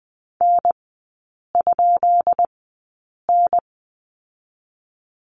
Morse code, heard as a keyed tone, about 20 wpm, spaced as for 8 wpm, 690 Hz. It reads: N?N
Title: 133 Lesson 22e - Question Mark - Three Characters 20wpm